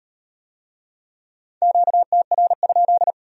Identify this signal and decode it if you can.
Morse code, heard as a keyed tone, about 37 wpm, 700 Hz. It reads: QTR?